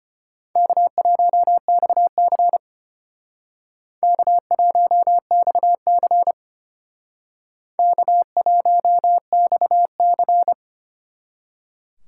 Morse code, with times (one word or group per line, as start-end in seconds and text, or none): 0.6-2.6 K1XC
4.0-6.3 K1XC
7.8-10.5 K1XC